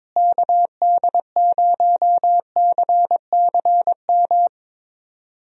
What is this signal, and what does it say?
Morse code, heard as a keyed tone, about 22 wpm, 700 Hz. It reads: KD0CCM
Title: USA Call Signs - 6 Characters - Repeated 2x - 22wpm